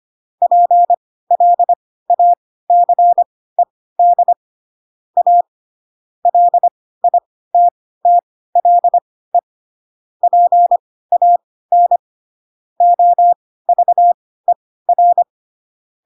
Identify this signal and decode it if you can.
Morse code, heard as a keyed tone, about 25 wpm, spaced as for 16 wpm, 700 Hz. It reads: PLACED A LITTLE PAN OVER